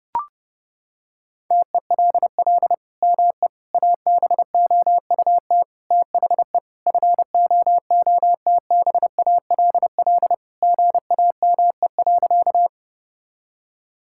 Morse code, heard as a keyed tone, about 32 wpm, 700 Hz. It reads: TELL ME ABOUT THE FOOTBALL GAME.
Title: Sentences from Top 1600 Words 30wpm